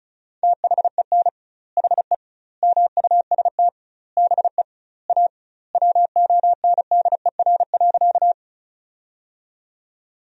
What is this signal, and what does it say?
Morse code, heard as a keyed tone, about 34 wpm, 700 Hz. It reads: THEN HE MUST BE A WONDER.
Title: 2x Word-Spacing - Sentences from Top 500 Words - 35wpm